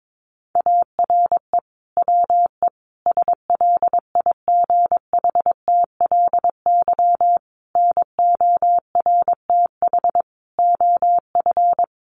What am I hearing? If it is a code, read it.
Morse code, heard as a keyed tone, about 22 wpm, 700 Hz. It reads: ARE WE SLIGHTLY NORTH OF
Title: Sentences from Top 1300 Words 22wpm